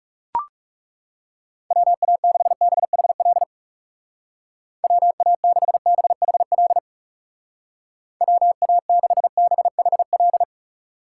Morse code, heard as a keyed tone, about 39 wpm, 700 Hz. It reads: WA6BHL WA6BHL WA6BHL